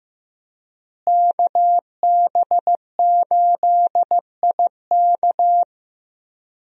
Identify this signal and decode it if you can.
Morse code, heard as a keyed tone, about 15 wpm, 700 Hz. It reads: KB8IK
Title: USA Call Signs - 5 Characters - Repeated 3x - 15wpm